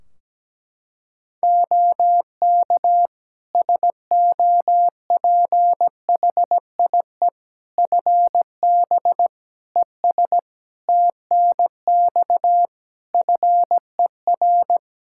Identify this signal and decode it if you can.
Morse code, heard as a keyed tone, about 17 wpm, 700 Hz. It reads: OK SOPHIE FB ES TNX FER
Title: Practice QSO - Exchange 2 - Info - 17wpm